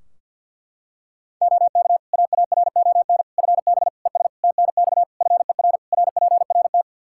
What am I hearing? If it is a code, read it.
Morse code, heard as a keyed tone, about 50 wpm, 700 Hz. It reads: OK AARON FB ES TNX FER RPRT
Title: Practice QSO - Exchange 2 - Info - 50wpm